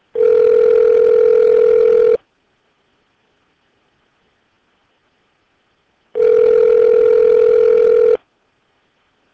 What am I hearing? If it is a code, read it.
Ringback tone (North America), 440 + 480 Hz.